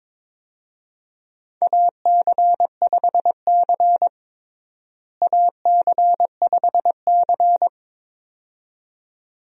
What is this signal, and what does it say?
Morse code, heard as a keyed tone, about 21 wpm, 700 Hz. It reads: AC5C AC5C
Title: USA Call Signs - 4 Characters - Repeated 2x - 22wpm